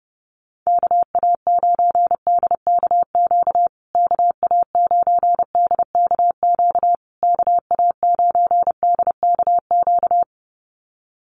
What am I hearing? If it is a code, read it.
Morse code, heard as a keyed tone, about 30 wpm, 700 Hz. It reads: KA9DKQ KA9DKQ KA9DKQ
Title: USA Call Signs Repeated 3x 30wpm